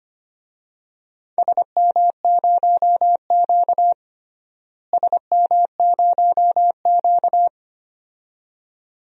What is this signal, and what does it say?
Morse code, heard as a keyed tone, about 25 wpm, 700 Hz. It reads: SM0Q SM0Q
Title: DXCC-100 Call Signs - 4 Characters - Repeated 2x - 25wpm